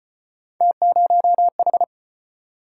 Morse code, heard as a keyed tone, about 35 wpm, 700 Hz. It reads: T0H